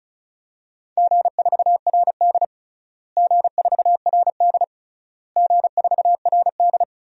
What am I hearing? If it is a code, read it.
Morse code, heard as a keyed tone, about 36 wpm, 700 Hz. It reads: G4RD G4RD G4RD
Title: DXCC-100 Call Signs - 4 Characters - Repeated 3x - 35wpm